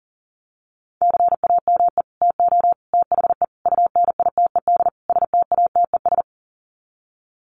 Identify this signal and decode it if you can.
Morse code, heard as a keyed tone, about 40 wpm, 700 Hz. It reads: CAME TO THE UNITED STATES
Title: Sets of 5 Words Encore2 40wpm